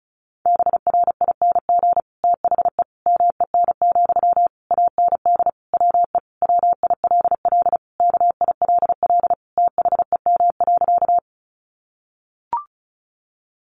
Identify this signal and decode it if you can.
Morse code, heard as a keyed tone, about 34 wpm, 700 Hz. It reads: BRING THE MEN, AND WE WILL KILL THEM.
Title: Sentences from Top 1400 Words 35wpm